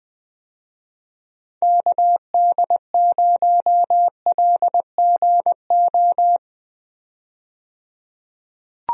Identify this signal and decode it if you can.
Morse code, heard as a keyed tone, about 21 wpm, 700 Hz. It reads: KD0LGO
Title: USA Call Signs - 6 Characters - 20wpm